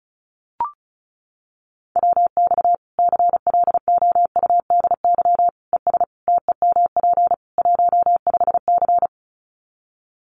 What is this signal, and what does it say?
Morse code, heard as a keyed tone, about 34 wpm, 700 Hz. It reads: WX CLOUDY ES TEMP 15C
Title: Common QSO Phrases 35wpm